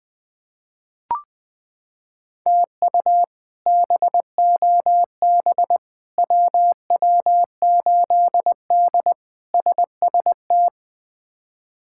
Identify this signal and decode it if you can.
Morse code, heard as a keyed tone, about 21 wpm, 700 Hz. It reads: TU BOB WW8D SST
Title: K1USN SST QSO Practice - 20wpm